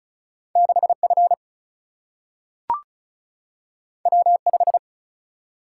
Morse code, heard as a keyed tone, about 36 wpm, 700 Hz. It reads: 6 F W 5
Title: Single-Letter Single-Number 35wpm